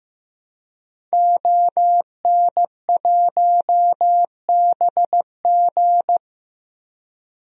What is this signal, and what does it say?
Morse code, heard as a keyed tone, about 15 wpm, 700 Hz. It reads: ON1BG
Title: DXCC-100 Call Signs - 5 Characters - Repeated 3x - 15wpm